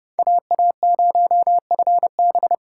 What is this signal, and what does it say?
Morse code, heard as a keyed tone, about 28 wpm, 700 Hz. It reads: AA0FB